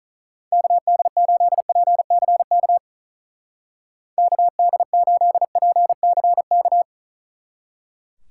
Morse code, heard as a keyed tone, about 44 wpm, 700 Hz. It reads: KD8PCK KD8PCK